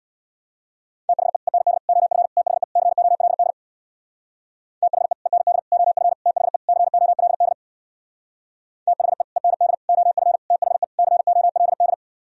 Morse code, heard as a keyed tone, about 62 wpm, 700 Hz. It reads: THE END OF THE COLD THE END OF THE COLD THE END OF THE COLD
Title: Speed-Racing - Sets of 5 Words 50wpm